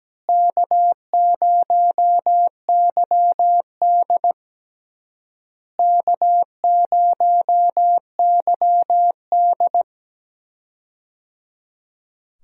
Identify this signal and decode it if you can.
Morse code, heard as a keyed tone, about 17 wpm, 700 Hz. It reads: K0YD K0YD
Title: USA Call Signs - 4 Characters - Repeated 3x - 17wpm